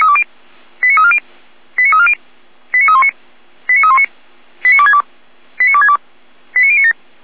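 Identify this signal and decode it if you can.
Five-tone selective call (ZVEI), heard as quick runs of five tones, tones of about 70 ms. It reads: 89340 89340 89319 89319 89262 89262 89097
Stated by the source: ZVEI five-tone selective call 89092 received on SDR.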